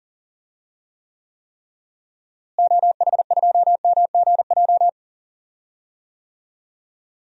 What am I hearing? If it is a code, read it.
Morse code, heard as a keyed tone, about 38 wpm, 700 Hz. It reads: OH2MGJ
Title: DXCC-100 Call Signs - 6 Characters - Repeated 3x - 40wpm